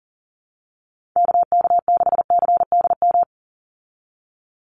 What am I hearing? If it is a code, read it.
Morse code, heard as a keyed tone, about 40 wpm, 700 Hz. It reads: KK6CDM
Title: USA Call Signs 40wpm